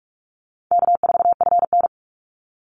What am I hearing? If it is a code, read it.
Morse code, heard as a keyed tone, about 44 wpm, 700 Hz. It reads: K4FN
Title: Easy Call Signs Repeated 3x - 45wpm